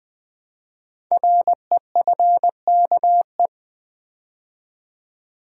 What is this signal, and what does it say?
Morse code, heard as a keyed tone, about 20 wpm, 700 Hz. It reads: REFKE
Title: Code Groups - 5 Letter:Numbers - 20wpm